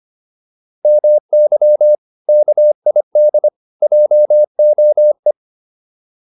Morse code, heard as a keyed tone, about 25 wpm, 600 Hz. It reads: MY KID JOE